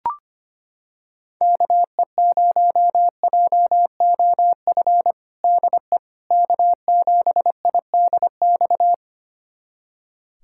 Morse code, heard as a keyed tone, about 25 wpm, 700 Hz. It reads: KE0JOF DE K7IDX
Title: USA Callsign DE Callsign 25wpm